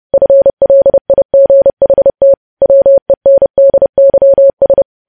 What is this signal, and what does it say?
Morse code, heard as a keyed tone, about 30 wpm, 550 Hz. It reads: FLIGHT WENDYS
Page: News Headlines in Morse Code at 30 WPM